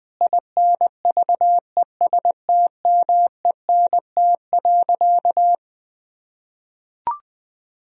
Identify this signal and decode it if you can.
Morse code, heard as a keyed tone, about 20 wpm, 700 Hz. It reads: INVESTMENT.
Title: Sentences from Top 1000 Words - Review Part 4 - 20wpm